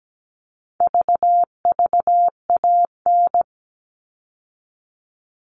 Morse code, heard as a keyed tone, about 17 wpm, 700 Hz. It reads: VVAN
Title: All Cut Numbers - 4-Digit 17wpm